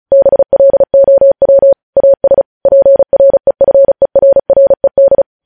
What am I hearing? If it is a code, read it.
Morse code, heard as a keyed tone, about 36 wpm, 550 Hz. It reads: BLOW AS PREFERRED
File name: podcast / News headlines-1x-35x35